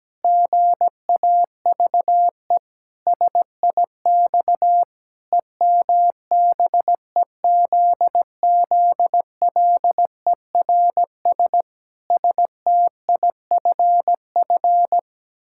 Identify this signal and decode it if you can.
Morse code, heard as a keyed tone, about 17 wpm, 700 Hz. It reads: GAVE SIX EMBEZZLERS STIFF
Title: Pangram Challenge v2 - 17wpm